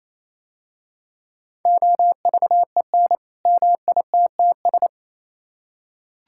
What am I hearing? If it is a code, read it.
Morse code, heard as a keyed tone, about 28 wpm, 700 Hz. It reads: OVEN MITTS